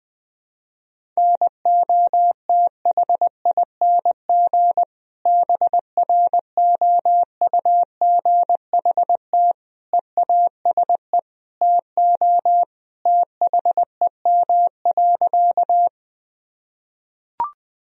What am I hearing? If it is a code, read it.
Morse code, heard as a keyed tone, about 20 wpm, 700 Hz. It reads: NOTHING BROUGHT EASE TO THEM.
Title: Sentences from Top 1000 Words - Review Part 1 - 20wpm